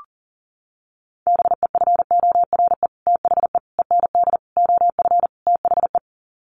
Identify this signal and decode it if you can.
Morse code, heard as a keyed tone, about 40 wpm, 700 Hz. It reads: BEFORE THE END OF THE
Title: Sets of 5 Words Encore 40wpm